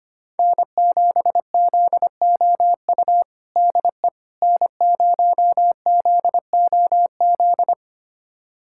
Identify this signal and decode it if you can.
Morse code, heard as a keyed tone, about 25 wpm, 700 Hz. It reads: N7ZOU DE N0ZOZ
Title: USA Callsign DE Callsign - 5 Characters - 25wpm